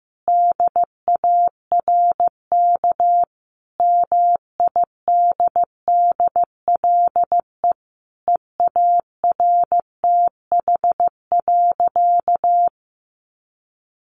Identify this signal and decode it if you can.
Morse code, heard as a keyed tone, about 15 wpm, 700 Hz. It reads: DARK MIDDLE EARTH.
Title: Sentences from Top 900 Words 15wpm